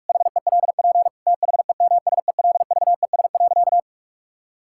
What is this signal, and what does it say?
Morse code, heard as a keyed tone, about 46 wpm, 700 Hz. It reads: HELP THEMSELVES.
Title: Sentences from Top 1000 Words - Review Part 5 - 45wpm